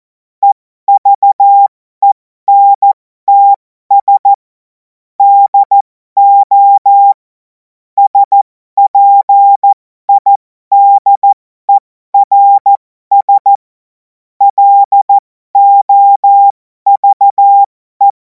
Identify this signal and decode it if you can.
Morse code, heard as a keyed tone, about 14 wpm, spaced as for 12 wpm, 800 Hz. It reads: EVENTS DO SPIDERS LOVE
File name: Morsekurs / Witze